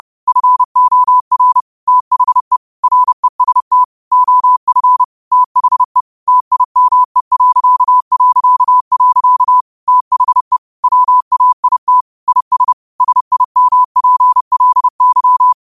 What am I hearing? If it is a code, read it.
Morse code, heard as a keyed tone, about 30 wpm, 1000 Hz. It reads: FOR THE REST OF THE TIME... THE WAIT IS SIMPLY